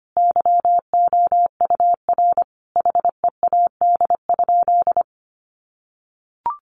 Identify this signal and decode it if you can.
Morse code, heard as a keyed tone, about 25 wpm, 700 Hz. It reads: YOUR HEAD?
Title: Sentences from Top 600 Words 25wpm